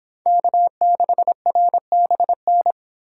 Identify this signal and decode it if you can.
Morse code, heard as a keyed tone, about 26 wpm, 700 Hz. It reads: K6RBN